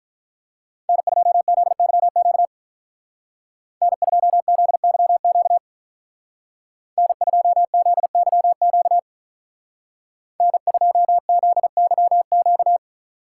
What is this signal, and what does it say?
Morse code, heard as a keyed tone, about 46 wpm, 700 Hz. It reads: N2ZYQ N2ZYQ N2ZYQ N2ZYQ